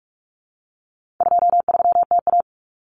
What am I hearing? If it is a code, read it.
Morse code, heard as a keyed tone, about 44 wpm, 700 Hz. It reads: 23TA
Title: Common Cut Numbers - 4-Digit 45wpm